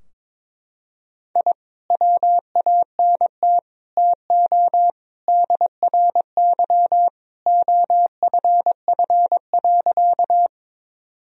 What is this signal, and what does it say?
Morse code, heard as a keyed tone, about 22 wpm, 700 Hz. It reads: I WANT TO DRY OFF.
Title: Sentences from Top 1000 Words - Review Part 3 - 22wpm